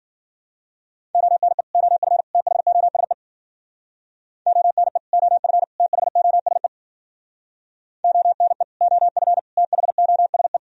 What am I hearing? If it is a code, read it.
Morse code, heard as a keyed tone, about 52 wpm, 700 Hz. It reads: ONE OF THOSE ONE OF THOSE ONE OF THOSE